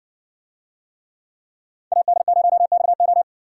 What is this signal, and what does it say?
Morse code, heard as a keyed tone, about 60 wpm, 700 Hz. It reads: AD0XO